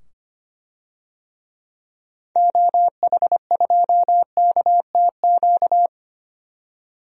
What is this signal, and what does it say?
Morse code, heard as a keyed tone, about 25 wpm, 700 Hz. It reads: OH2KTQ
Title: DXCC-100 Call Signs - 6 Characters - 25wpm